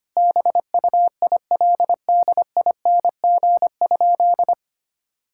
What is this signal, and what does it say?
Morse code, heard as a keyed tone, about 25 wpm, 700 Hz. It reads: BUILDING?
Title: Sentences from Top 1000 Words - Review Part 6 - 25wpm